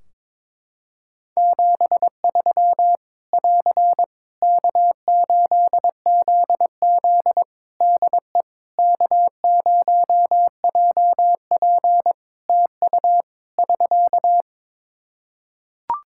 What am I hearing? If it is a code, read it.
Morse code, heard as a keyed tone, about 22 wpm, 700 Hz. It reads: 73 <AR> K8ZZ DE K0JP TU <SK>
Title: Practice QSO - Exchange 4 - Ending - 22wpm